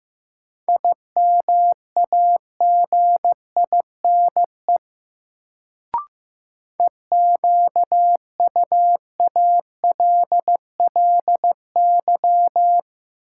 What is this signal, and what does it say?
Morse code, heard as a keyed tone, about 15 wpm, 700 Hz. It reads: IMAGINE EQUALLY
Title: Sentences from Top 1400 Words - New Word Drill - 15wpm